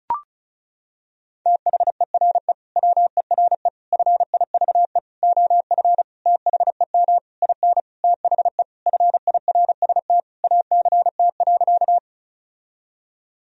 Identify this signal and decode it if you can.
Morse code, heard as a keyed tone, about 34 wpm, 700 Hz. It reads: THERE WERE FIVE OF THEM IN THE FIRST ACT.
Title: Sentences from Top 1000 Words - Review Part 2 - 35wpm